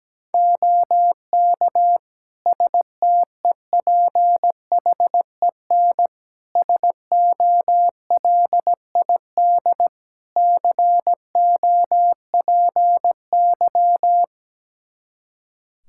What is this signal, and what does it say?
Morse code, heard as a keyed tone, about 17 wpm, 700 Hz. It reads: OK STEPHEN SOLID COPY
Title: K1USN SST QSO Practice 17wpm